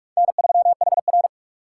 Morse code, heard as a keyed tone, about 44 wpm, 700 Hz. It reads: N3HR